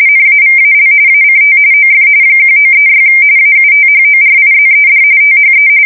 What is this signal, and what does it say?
RTTY, about 45.45 baud. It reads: NAVAL ACT OF 1936 AUTHORIZED THE
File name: fsk170 / usnavy4